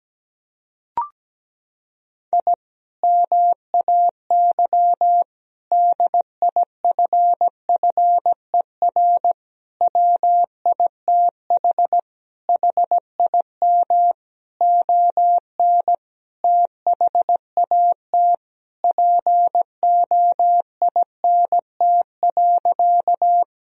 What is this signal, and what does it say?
Morse code, heard as a keyed tone, about 17 wpm, 700 Hz. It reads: I MAY DIFFER WITH HIM ON THAT POINT.
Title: Sentences from Top 1000 Words - Review Part 1 - 17wpm